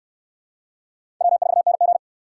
Morse code, heard as a keyed tone, about 62 wpm, 700 Hz. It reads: W4NG